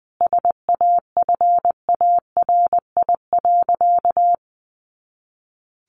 Morse code, heard as a keyed tone, about 20 wpm, 700 Hz. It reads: SAFARI.
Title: A Sound Of Thunder 20wpm